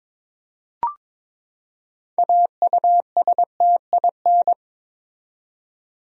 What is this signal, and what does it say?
Morse code, heard as a keyed tone, about 22 wpm, 700 Hz. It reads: AUSTIN
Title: USA Common Names - 22wpm